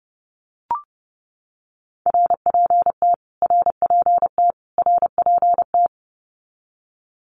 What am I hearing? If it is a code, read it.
Morse code, heard as a keyed tone, about 30 wpm, 700 Hz. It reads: RPT RPT RPT
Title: Abbreviations Repeated 3x - 30wpm